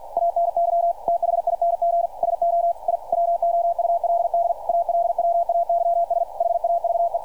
Morse code, heard as a keyed tone, about 48 wpm, 690 Hz, with radio noise. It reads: ANYTHING SO I WOULD ALWAYS HAVE
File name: archived / CRSnet